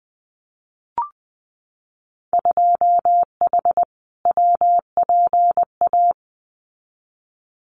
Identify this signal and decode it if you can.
Morse code, heard as a keyed tone, about 20 wpm, 700 Hz. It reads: 2H WPA
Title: Winter Field Day Exchanges 20wpm